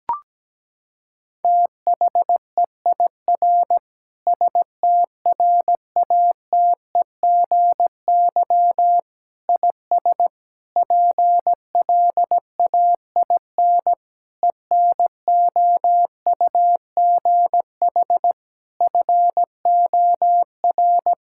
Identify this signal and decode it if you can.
Morse code, heard as a keyed tone, about 17 wpm, 700 Hz. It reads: THEIR STRATEGY IS PLAIN ENOUGH FOR
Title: Sentences from Top 1000 Words - Review Part 5 - 17wpm